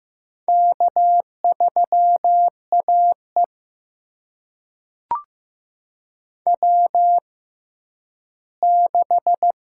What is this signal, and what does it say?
Morse code, heard as a keyed tone, about 15 wpm, spaced as for 10 wpm, 700 Hz. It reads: K3AE W6
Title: USA Call Signs - 4 Characters - 6x Character Spacing - 15wpm